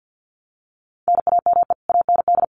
Morse code, heard as a keyed tone, about 50 wpm, 700 Hz. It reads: NAME AND
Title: Sentences from Top 1000 Words 50wpm